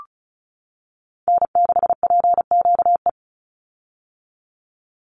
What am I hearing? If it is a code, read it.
Morse code, heard as a keyed tone, about 34 wpm, 700 Hz. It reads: N6PQE
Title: USA Call Signs 35wpm